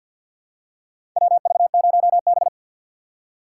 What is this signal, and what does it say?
Morse code, heard as a keyed tone, about 46 wpm, 700 Hz. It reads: WV0B
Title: USA Call Signs - 4 Characters - 2x Character Spacing - 50wpm